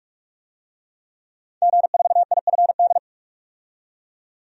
Morse code, heard as a keyed tone, about 46 wpm, 700 Hz. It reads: G4IFD
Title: DXCC-100 Call Signs - 5 Characters - 45wpm